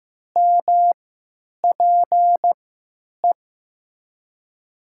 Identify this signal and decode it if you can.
Morse code, heard as a keyed tone, about 15 wpm, spaced as for 9 wpm, 700 Hz. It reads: MPE
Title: USA Call Signs - 3x Character Spacing - 15wpm